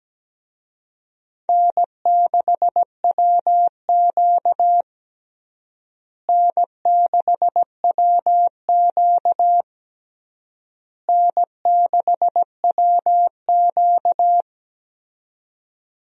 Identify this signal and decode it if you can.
Morse code, heard as a keyed tone, about 17 wpm, 700 Hz. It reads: N6WQ N6WQ N6WQ
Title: USA Call Signs - 4 Characters - Repeated 3x - 17wpm